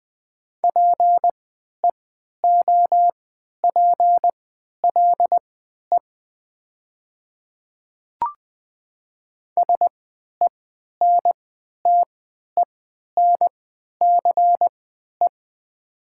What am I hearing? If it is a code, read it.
Morse code, heard as a keyed tone, about 20 wpm, spaced as for 11 wpm, 700 Hz. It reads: PEOPLE SENTENCE
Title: 3x Character Spacing - Top 500 Words - 20wpm